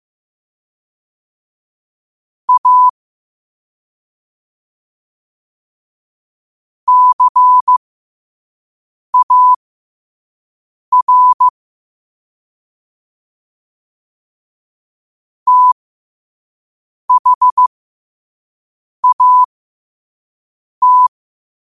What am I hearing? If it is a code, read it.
Morse code, heard as a keyed tone, about 15 wpm, spaced as for 5 wpm, 1000 Hz. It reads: A CAR THAT